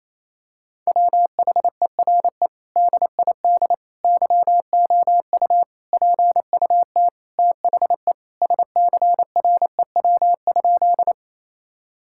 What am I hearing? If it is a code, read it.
Morse code, heard as a keyed tone, about 28 wpm, 700 Hz. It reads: WHERE DID YOU PUT THE SCREW?
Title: Sentences from Top 1600 Words 28wpm